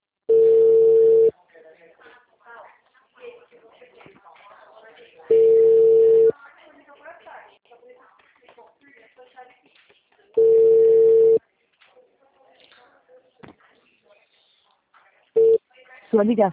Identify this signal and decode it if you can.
Ringback tone (China, 450 Hz), with speech.
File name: out- -2006- - - .724